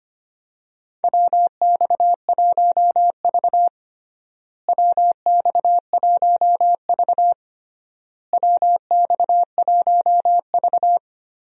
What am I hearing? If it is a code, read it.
Morse code, heard as a keyed tone, about 25 wpm, 700 Hz. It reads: WX1V WX1V WX1V
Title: USA Call Signs - 4 Characters - Repeated 3x - 25wpm